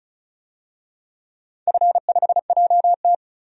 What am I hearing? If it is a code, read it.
Morse code, heard as a keyed tone, about 36 wpm, 700 Hz. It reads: F5JT
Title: DXCC-100 Call Signs - 4 Characters - 35wpm